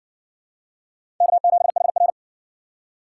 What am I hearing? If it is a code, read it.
Morse code, heard as a keyed tone, about 60 wpm, 700 Hz. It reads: K7HR